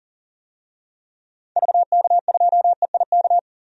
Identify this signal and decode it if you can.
Morse code, heard as a keyed tone, about 40 wpm, 700 Hz. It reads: VK2EIK
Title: DXCC-100 Call Signs - 6 Characters - 40wpm